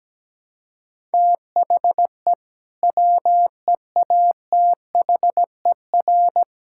Morse code, heard as a keyed tone, about 17 wpm, 700 Hz. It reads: THE WEATHER
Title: Sentences from Top 1000 Words - Review Part 5 - 17wpm